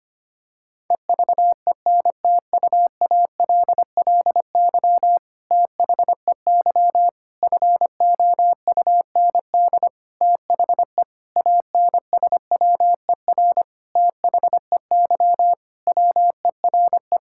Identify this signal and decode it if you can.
Morse code, heard as a keyed tone, about 25 wpm, 700 Hz. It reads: EVENTUALLY THEY FOUND THE ANSWER THEY WERE
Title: Sentences from Top 1000 Words - Review Part 6 - 25wpm